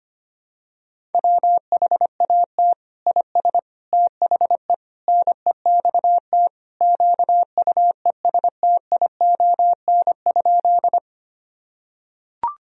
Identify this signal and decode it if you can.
Morse code, heard as a keyed tone, about 25 wpm, 700 Hz. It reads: WHAT IS THE NEXT QUESTION?
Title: Sentences from Top 1000 Words - Review Part 1 - 25wpm